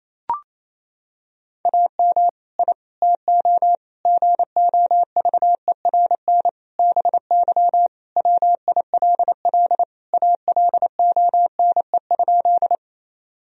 Morse code, heard as a keyed tone, about 28 wpm, 700 Hz. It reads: AM I TO GOVERN BY WILL ALONE?
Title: Sentences from Top 1000 Words - Review Part 2 - 28wpm